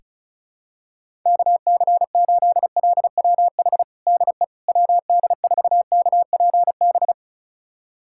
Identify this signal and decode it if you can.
Morse code, heard as a keyed tone, about 36 wpm, 700 Hz. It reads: KC8LWH DE WD4KPB
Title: USA Callsign DE Callsign 35wpm